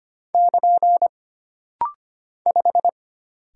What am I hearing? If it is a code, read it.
Morse code, heard as a keyed tone, about 25 wpm, 700 Hz. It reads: <KN> 5
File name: Single Character (Letter:Number:Punctuation:Prosign) - Rapid-Fire - 25wpm